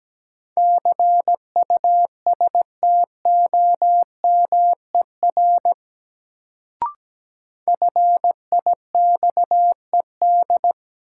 Morse code, heard as a keyed tone, about 17 wpm, 700 Hz. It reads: CUSTOMER FIXED
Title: Sentences from Top 1100 Words - New Word Drill - 17wpm